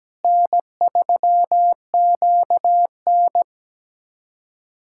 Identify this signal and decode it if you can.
Morse code, heard as a keyed tone, about 17 wpm, 700 Hz. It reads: N3QN
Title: USA Callsign DE Callsign - 4 Characters - 17wpm